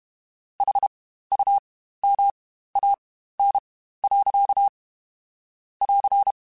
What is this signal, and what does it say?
Morse code, heard as a keyed tone, about 31 wpm, spaced as for 15 wpm, 800 Hz. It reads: HUMAN. <AR>
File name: Spaced / Chuckles Spaced-mp3s / HumanErr sp @30